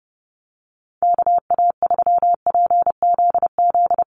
Morse code, heard as a keyed tone, about 30 wpm, 700 Hz. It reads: KA3PZZ
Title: USA Call Signs Repeated 2x 30wpm